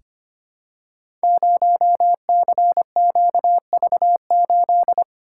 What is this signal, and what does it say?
Morse code, heard as a keyed tone, about 25 wpm, 700 Hz. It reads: 0CQV8
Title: Code Groups - 5 Letter:Numbers - 25wpm